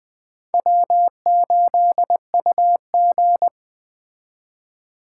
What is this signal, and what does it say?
Morse code, heard as a keyed tone, about 20 wpm, 700 Hz. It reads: W8UG
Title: Speed Racing - USA Call Signs 17wpm